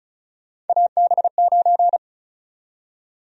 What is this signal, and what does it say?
Morse code, heard as a keyed tone, about 36 wpm, 700 Hz. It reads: AB9
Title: USA Call Sign Prefixes - 35wpm